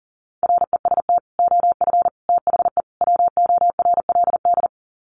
Morse code, heard as a keyed tone, about 40 wpm, 700 Hz. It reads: REST OF THE WORLD
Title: Sets of 4 Words 40wpm